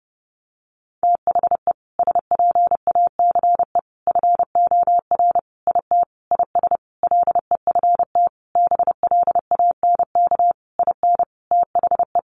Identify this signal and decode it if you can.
Morse code, heard as a keyed tone, about 30 wpm, 700 Hz. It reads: THE SPACE FOR IT IS LEFT BLANK IN THE
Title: Sentences from Top 1500 Words 30wpm